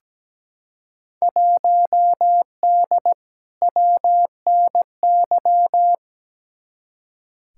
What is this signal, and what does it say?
Morse code, heard as a keyed tone, about 17 wpm, 700 Hz. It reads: 1D WNY
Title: Field Day Exchanges - 17wpm